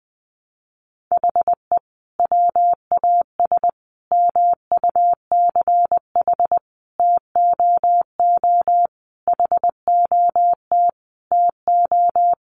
Morse code, heard as a keyed tone, about 20 wpm, 700 Hz. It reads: HE WAS MUCH TOO HOT TO
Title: Sentences from Top 1200 Words 20wpm